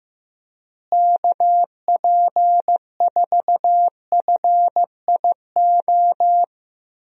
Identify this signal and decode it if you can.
Morse code, heard as a keyed tone, about 16 wpm, 700 Hz. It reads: KP4FIO
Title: USA Call Signs - 6 Characters - Repeated 3x - 15wpm